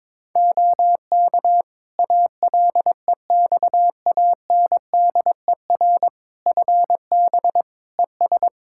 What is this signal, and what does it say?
Morse code, heard as a keyed tone, about 21 wpm, 700 Hz. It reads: OK ALEXANDER FB ES
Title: Practice QSO - Exchange 2 - Info - 22wpm